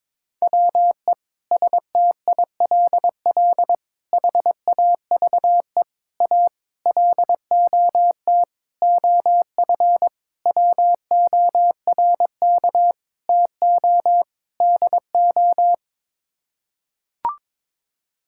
Morse code, heard as a keyed tone, about 22 wpm, 700 Hz. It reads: WE STILL HAVE A LOT OF WORK TO DO
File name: Sets of 9 Words - 22wpm